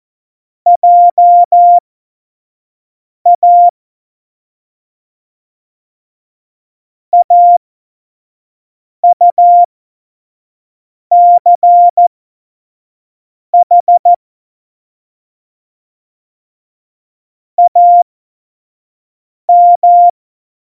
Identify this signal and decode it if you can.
Morse code, heard as a keyed tone, about 14 wpm, spaced as for 5 wpm, 700 Hz. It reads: JA AUCH AM
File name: Morsekurs / Witze / Witz 17.Dezember 2019, 25 F70